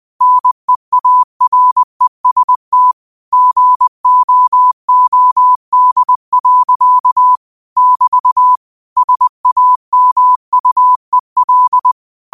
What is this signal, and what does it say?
Morse code, heard as a keyed tone, about 20 wpm, 1000 Hz. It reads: NEAREST GOOD. <BT> SAMUEL
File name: qotdfiles / 20wpm